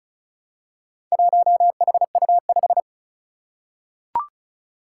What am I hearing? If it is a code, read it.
Morse code, heard as a keyed tone, about 36 wpm, 700 Hz. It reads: 1HU5
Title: Four Challenging Letter-Number Combinations 35wpm